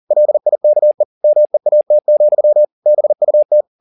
Morse code, heard as a keyed tone, about 40 wpm, 600 Hz. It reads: LIKE MEAT, BUT
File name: the old man and the sea 2 40wpm